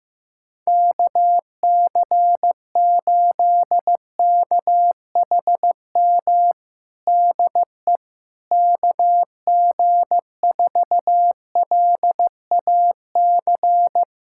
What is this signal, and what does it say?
Morse code, heard as a keyed tone, about 15 wpm, 700 Hz. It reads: KC8KHM DE KG4LAC
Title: USA Callsign DE Callsign - 6 Characters - 15wpm